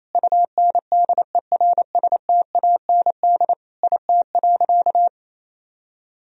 Morse code, heard as a keyed tone, about 28 wpm, 700 Hz. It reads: UNDERSTAND IT.
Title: Sentences from Top 1000 Words - Review Part 4 - 28wpm